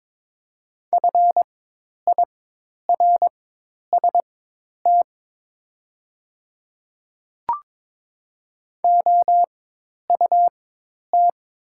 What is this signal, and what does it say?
Morse code, heard as a keyed tone, about 22 wpm, spaced as for 10 wpm, 700 Hz. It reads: FIRST OUT
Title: 4x Character Spacing - Top 100 Words - 22wpm